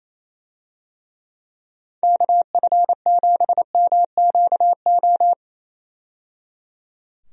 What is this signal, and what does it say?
Morse code, heard as a keyed tone, about 27 wpm, 700 Hz. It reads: KF7MQO
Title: USA Call Signs - 6 Characters - 28wpm